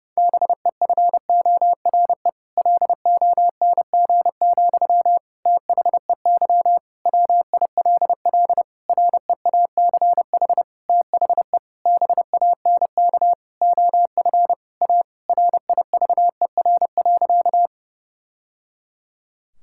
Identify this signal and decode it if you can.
Morse code, heard as a keyed tone, about 32 wpm, 700 Hz. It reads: BEFORE LONG, THEY WILL REACH THE BANK OF A RIVER.
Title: Sentences from Top 1000 Words - Review Part 6 - 30wpm